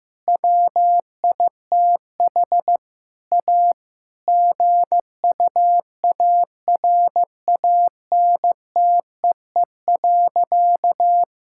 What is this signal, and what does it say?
Morse code, heard as a keyed tone, about 15 wpm, 700 Hz. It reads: WITH A GUARANTEE.
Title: Sentences from Top 1600 Words 15wpm